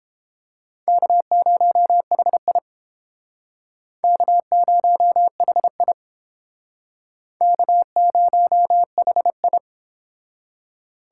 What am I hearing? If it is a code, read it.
Morse code, heard as a keyed tone, about 29 wpm, 700 Hz. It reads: K0HI K0HI K0HI